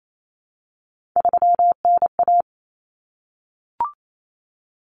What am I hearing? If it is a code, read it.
Morse code, heard as a keyed tone, about 28 wpm, 700 Hz. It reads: 3NA
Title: Common Cut Numbers - 3-Digit 28wpm